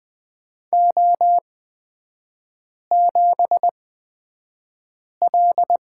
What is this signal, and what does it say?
Morse code, heard as a keyed tone, about 20 wpm, spaced as for 5 wpm, 690 Hz. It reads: O7L
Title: 229 Lesson 37e - Letter x - Three Characters 20wpm